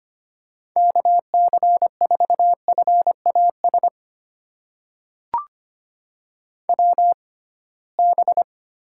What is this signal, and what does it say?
Morse code, heard as a keyed tone, about 25 wpm, spaced as for 19 wpm, 700 Hz. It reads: KC4FAS WB
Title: USA Call Signs - 6 Characters - 6x Character Spacing - 25wpm